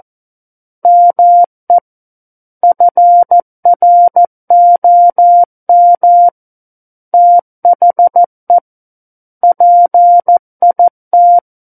Morse code, heard as a keyed tone, about 14 wpm, 700 Hz. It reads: ME FROM THE PIT
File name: Book One / 05-Chapter 5 - 14 WPM